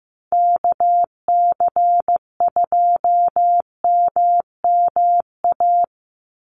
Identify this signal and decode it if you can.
Morse code, heard as a keyed tone, about 15 wpm, 700 Hz. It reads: KC2MMA
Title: USA Call Signs 15wpm